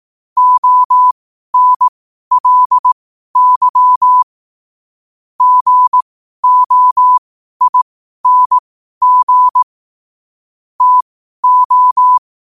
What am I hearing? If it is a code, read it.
Morse code, heard as a keyed tone, about 18 wpm, spaced as for 12 wpm, 1000 Hz. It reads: ONLY GOING TO
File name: qotdfiles / 12wpm